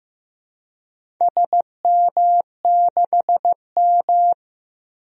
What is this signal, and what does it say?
Morse code, heard as a keyed tone, about 15 wpm, 700 Hz. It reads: SM6M
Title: DXCC-100 Call Signs - 4 Characters - 15wpm